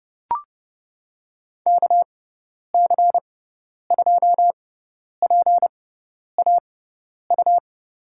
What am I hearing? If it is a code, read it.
Morse code, heard as a keyed tone, about 30 wpm, spaced as for 10 wpm, 700 Hz. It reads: KC2PAU
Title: USA Call Signs - 6 Characters - 6x Character Spacing - 30wpm